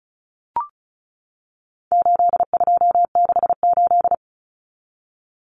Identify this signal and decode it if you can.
Morse code, heard as a keyed tone, about 34 wpm, 700 Hz. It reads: 8268
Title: Numbers - 4-Digit 35wpm